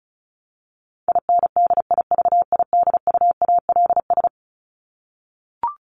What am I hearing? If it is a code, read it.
Morse code, heard as a keyed tone, about 34 wpm, 700 Hz. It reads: INDIVIDUALS
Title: ES:S Suffix - Top 100 Words with Base Words 35wpm